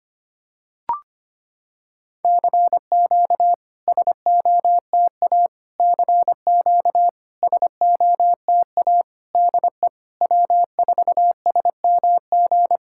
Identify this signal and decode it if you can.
Morse code, heard as a keyed tone, about 25 wpm, 700 Hz. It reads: CQ SOTA CQ SOTA DE W4SMG
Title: SOTA QSO Practice 25wpm